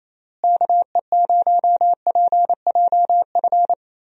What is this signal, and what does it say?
Morse code, heard as a keyed tone, about 27 wpm, 700 Hz. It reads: KE0PJF
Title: USA Call Signs - 6 Characters - Repeated 3x - 28wpm